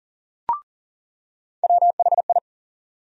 Morse code, heard as a keyed tone, about 40 wpm, 700 Hz. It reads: WHI
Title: USA Call Sign Postfixes - 3 Letters 40wpm